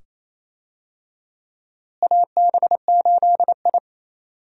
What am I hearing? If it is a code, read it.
Morse code, heard as a keyed tone, about 28 wpm, 700 Hz. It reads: AB8I
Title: USA Call Signs - 4 Characters - Repeated 2x - 28wpm